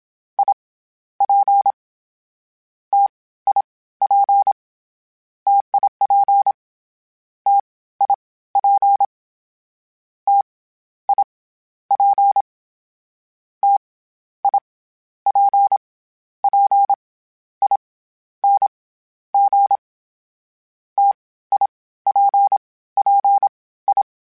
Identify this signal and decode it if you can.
Morse code, heard as a keyed tone, about 27 wpm, 800 Hz. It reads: I P T I P TIP T I P T I P T I P P I N G T I P P I
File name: Variable Spacing / Variable Spacing-mp3s / T AOEs @25